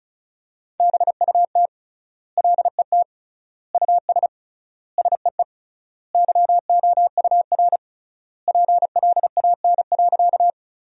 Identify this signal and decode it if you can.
Morse code, heard as a keyed tone, about 34 wpm, 700 Hz. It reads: BUT LET US SEE YOUR PLAN.
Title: 3x Word-Spacing - Sentences from Top 500 Words - 35wpm